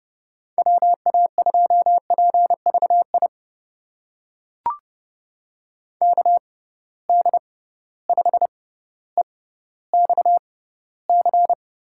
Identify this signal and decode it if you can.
Morse code, heard as a keyed tone, about 30 wpm, spaced as for 15 wpm, 700 Hz. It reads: WA2PVI KD5EXC
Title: USA Call Signs - 6 Characters - 6x Character Spacing - 30wpm